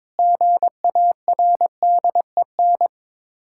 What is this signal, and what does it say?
Morse code, heard as a keyed tone, about 22 wpm, 690 Hz. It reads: GARDEN
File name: Difficult Words with Hints - 22wpm